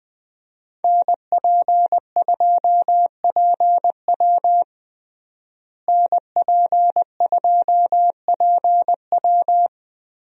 Morse code, heard as a keyed tone, about 20 wpm, 700 Hz. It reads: NP2PW NP2PW
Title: USA Call Signs - 5 Characters - Repeated 3x - 20wpm